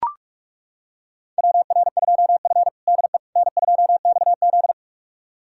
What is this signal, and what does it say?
Morse code, heard as a keyed tone, about 41 wpm, 700 Hz. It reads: WR2F DE N2XZ
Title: USA Callsign DE Callsign - 4 Characters - 45wpm